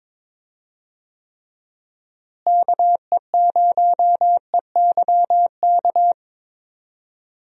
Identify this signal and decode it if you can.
Morse code, heard as a keyed tone, about 21 wpm, 700 Hz. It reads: KE0EYK
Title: USA Call Signs - 6 Characters - Repeated 3x - 22wpm